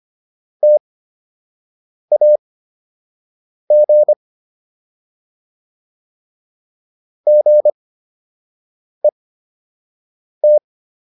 Morse code, heard as a keyed tone, about 25 wpm, spaced as for 6 wpm, 600 Hz. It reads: TAG GET